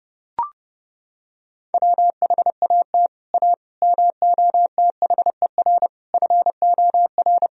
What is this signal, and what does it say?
Morse code, heard as a keyed tone, about 30 wpm, 700 Hz. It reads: WHAT A MOTHER FOR
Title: Sentences from Top 1000 Words - Review Part 1 - 30wpm